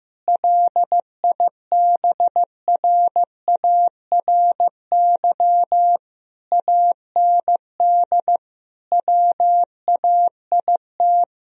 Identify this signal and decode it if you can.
Morse code, heard as a keyed tone, about 15 wpm, 700 Hz. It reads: LIBRARY AND WAIT
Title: Sentences from Top 1000 Words - Review Part 4 - 15wpm